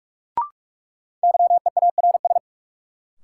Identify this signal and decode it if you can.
Morse code, heard as a keyed tone, about 46 wpm, 700 Hz. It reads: YEARS
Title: Sets of 2 Words - New Word Drill - 45wpm